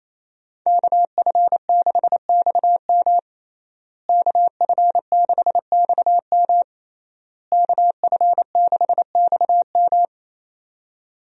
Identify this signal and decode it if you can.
Morse code, heard as a keyed tone, about 28 wpm, 700 Hz. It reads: KF6XM KF6XM KF6XM
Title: USA Call Signs - 5 Characters - Repeated 3x - 28wpm